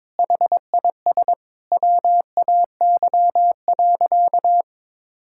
Morse code, heard as a keyed tone, about 22 wpm, 700 Hz. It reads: HIS WAY.
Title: Sentences from Top 1000 Words - Review Part 5 - 22wpm